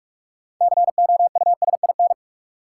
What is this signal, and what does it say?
Morse code, heard as a keyed tone, about 43 wpm, 700 Hz. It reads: COUSIN
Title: Six Letter Words - 45wpm